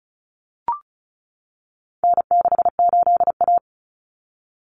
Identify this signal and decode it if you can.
Morse code, heard as a keyed tone, about 34 wpm, 700 Hz. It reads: N68A
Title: Common Cut Numbers - 4-Digit 35wpm